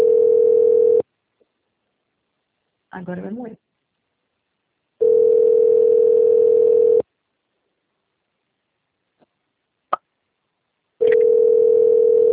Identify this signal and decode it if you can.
Ringback tone (North America), 440 + 480 Hz, with speech.
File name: out- - - - - .16452